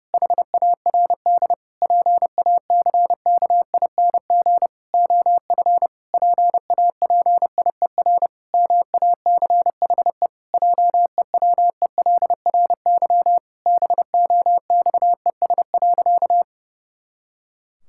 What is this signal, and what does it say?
Morse code, heard as a keyed tone, about 32 wpm, 700 Hz. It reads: HARD PACKING OF PAPIER MACHE JEWELRY BOXES.
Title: Pangram Challenge v2 - 30wpm